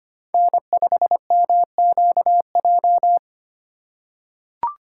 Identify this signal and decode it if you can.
Morse code, heard as a keyed tone, about 25 wpm, 700 Hz. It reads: N5MQJ
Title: USA Call Signs - 5 Characters - 4x Character Spacing - 25wpm